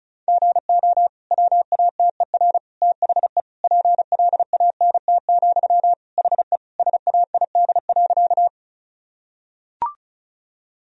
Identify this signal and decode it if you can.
Morse code, heard as a keyed tone, about 36 wpm, 700 Hz. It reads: GO WATER THE PLANT, HE SAID.
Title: Sentences from Top 1000 Words - Review Part 1 - 35wpm